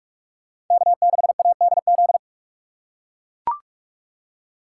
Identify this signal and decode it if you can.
Morse code, heard as a keyed tone, about 41 wpm, 700 Hz. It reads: K6ADZ
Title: USA Call Signs - 2x Character Spacing - 45wpm